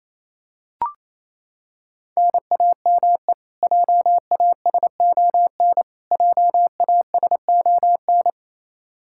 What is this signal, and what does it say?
Morse code, heard as a keyed tone, about 28 wpm, 700 Hz. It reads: NAME JASON JASON
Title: Practice QSO - Exchange 1 - Beginning - 28wpm